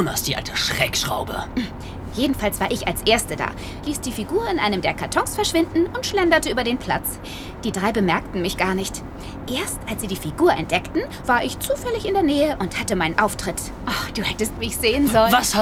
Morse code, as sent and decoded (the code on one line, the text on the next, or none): none